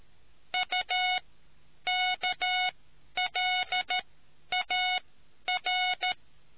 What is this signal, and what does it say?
Morse code, heard as a keyed tone, about 13 wpm, spaced as for 9 wpm, 730 Hz. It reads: UKLAR